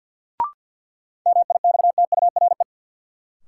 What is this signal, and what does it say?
Morse code, heard as a keyed tone, about 53 wpm, 700 Hz. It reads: MIXTURE
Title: Sentences from Top 1500 Words - New Word Drill Alt - 50wpm